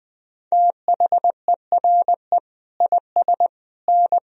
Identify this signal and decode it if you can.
Morse code, heard as a keyed tone, about 21 wpm, 700 Hz. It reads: THERE IS N
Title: Sentences from Top 200 Words and Abbreviations 20wpm